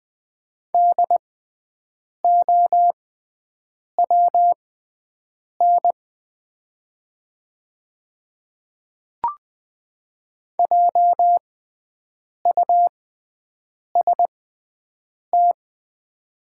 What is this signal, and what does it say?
Morse code, heard as a keyed tone, about 20 wpm, spaced as for 7 wpm, 700 Hz. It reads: DOWN JUST
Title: 6x Character Spacing - Top 200 Words - 20wpm